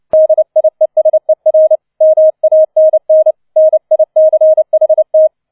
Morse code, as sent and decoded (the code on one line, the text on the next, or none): -.. .. ... .-. / -- .- -. -. / -. .. -.-. .... -
DIESER MANN NICHT